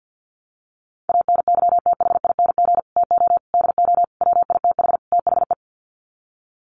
Morse code, heard as a keyed tone, about 50 wpm, 700 Hz. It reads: ANYTHING TO DO WITH THE
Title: Sets of 5 Words Encore 50wpm